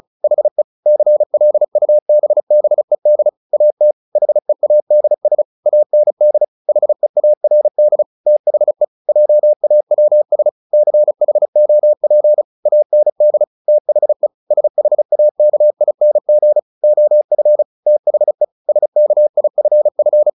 Morse code, heard as a keyed tone, about 34 wpm, 600 Hz. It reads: HE CLUBBED AT HEADS AND HEARD THE JAWS CHOP AND THE SHAKING OF THE SKIFF
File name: the old man and the sea 2 35wpm